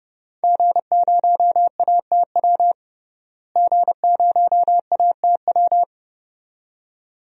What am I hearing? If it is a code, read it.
Morse code, heard as a keyed tone, about 32 wpm, 700 Hz. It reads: G0ATW G0ATW
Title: DXCC-100 Call Signs - 5 Characters - Repeated 2x - 30wpm